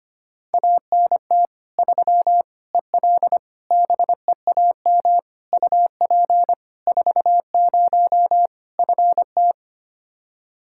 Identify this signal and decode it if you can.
Morse code, heard as a keyed tone, about 25 wpm, 700 Hz. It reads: ANT 3 EL BEAM UP 40 FT
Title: Practice QSO - Exchange 2 - Info - 25wpm